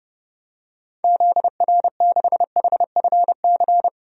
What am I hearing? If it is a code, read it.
Morse code, heard as a keyed tone, about 31 wpm, 700 Hz. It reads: ZR6HFC
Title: DXCC-100 Call Signs - 6 Characters - 30wpm